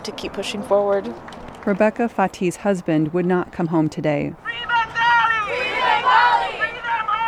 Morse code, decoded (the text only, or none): none